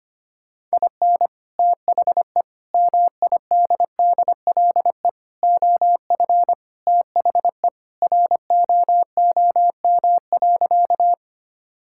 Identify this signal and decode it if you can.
Morse code, heard as a keyed tone, about 25 wpm, 690 Hz. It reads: IN THE MIDDLE OF THE ROOM.
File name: practice / 25wpm / Sentences with 2-Word Phrases - 25wpm